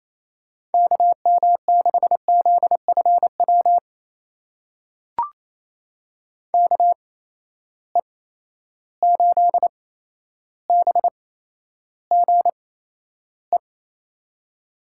Morse code, heard as a keyed tone, about 28 wpm, spaced as for 12 wpm, 700 Hz. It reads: KM6ZFW KE8BGE
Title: USA Call Signs - 6 Characters - 8x Character Spacing - 28wpm